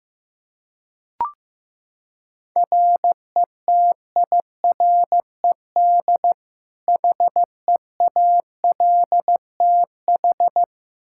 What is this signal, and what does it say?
Morse code, heard as a keyed tone, about 15 wpm, 700 Hz. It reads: RETIRED HEALTH
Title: Practice QSO - Exchange 3 - More Info - 15wpm